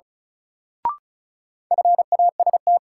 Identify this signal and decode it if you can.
Morse code, heard as a keyed tone, about 36 wpm, 700 Hz. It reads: FAST
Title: Sentences from Top 400 Words - New Word Drill - 35wpm